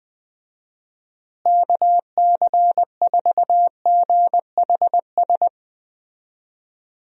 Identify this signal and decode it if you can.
Morse code, heard as a keyed tone, about 20 wpm, 700 Hz. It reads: KC4GHS
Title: USA Call Signs - 6 Characters - Repeated 2x - 20wpm